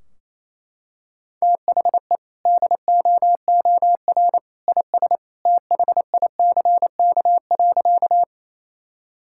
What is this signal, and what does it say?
Morse code, heard as a keyed tone, about 28 wpm, 700 Hz. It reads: THE DOOR IS THICK.
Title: Sentences from Top 1600 Words 28wpm